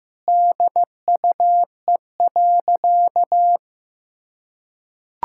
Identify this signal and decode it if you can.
Morse code, heard as a keyed tone, about 15 wpm, 700 Hz. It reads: DUE.
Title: Sentences from Top 1000 Words - Review Part 6 - 15wpm